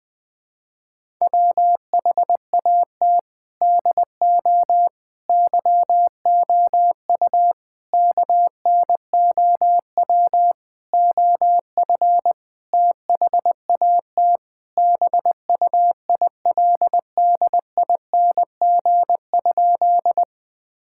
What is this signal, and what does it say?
Morse code, heard as a keyed tone, about 21 wpm, 700 Hz. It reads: WHAT DO YOU KNOW OF THAT BUILDING?
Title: Sentences from Top 1000 Words - Review Part 5 - 20wpm